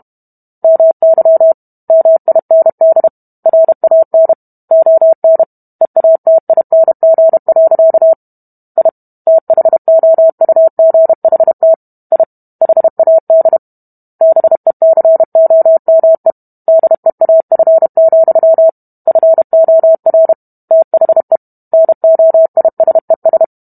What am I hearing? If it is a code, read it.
Morse code, heard as a keyed tone, about 32 wpm, 650 Hz. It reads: MY MIND RAN ON EATING. I THOUGHT I HAD BECOME DEAF, FOR THE NOISES